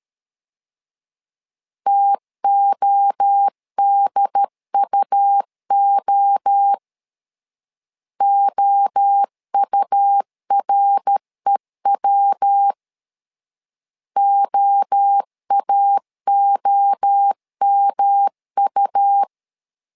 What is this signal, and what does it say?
Morse code, heard as a keyed tone, about 13 wpm, 800 Hz. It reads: TODUO OUREW OAOMU
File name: LEZIONI / 3 MWRO / 3 MWRO 12wpm 1730